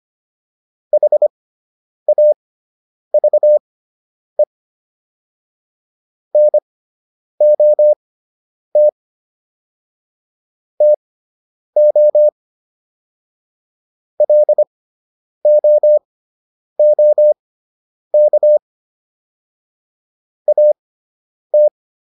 Morse code, heard as a keyed tone, about 25 wpm, spaced as for 9 wpm, 600 Hz. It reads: HAVE NOT TO LOOK AT